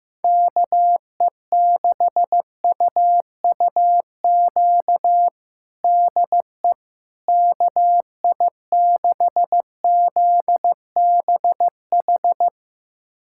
Morse code, heard as a keyed tone, about 16 wpm, 700 Hz. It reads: KE6UUQ DE KI6ZBH
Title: USA Callsign DE Callsign 15wpm